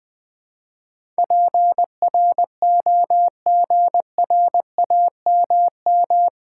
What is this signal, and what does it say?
Morse code, heard as a keyed tone, about 20 wpm, 700 Hz. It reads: PROGRAMM
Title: Build A Word - 4 Letters At A Time - 11-Letter Words - 20wpm